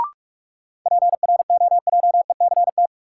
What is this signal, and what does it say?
Morse code, heard as a keyed tone, about 46 wpm, 700 Hz. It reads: PROJECT
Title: Sentences from Top 1400 Words - New Word Drill Alt - 45wpm